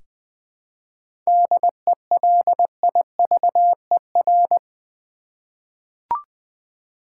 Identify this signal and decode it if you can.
Morse code, heard as a keyed tone, about 20 wpm, 700 Hz. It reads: DELIVER
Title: Seven Letter Words - 20wpm